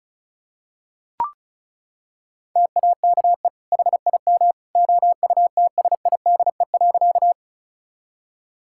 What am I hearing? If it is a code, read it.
Morse code, heard as a keyed tone, about 34 wpm, 700 Hz. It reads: TAKE HIM OUTSIDE.
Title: Sentences from Top 1000 Words - Review Part 6 - 35wpm